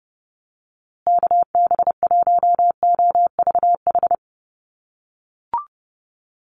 Morse code, heard as a keyed tone, about 30 wpm, 700 Hz. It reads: KB1OVH
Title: USA Call Signs Repeated 2x 30wpm